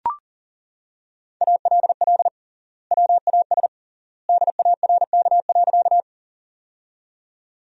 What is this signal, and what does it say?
Morse code, heard as a keyed tone, about 40 wpm, 700 Hz. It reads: ALL WAS DARK.
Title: 3x Word-Spacing - Sentences from Top 500 Words - 40wpm